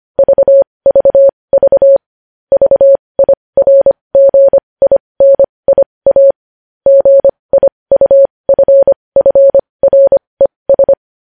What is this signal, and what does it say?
Morse code, heard as a keyed tone, about 25 wpm, spaced as for 20 wpm, 550 Hz. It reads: VVV VIRGINIA GIUFFRES
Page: news headlines in morse code at 20 wpm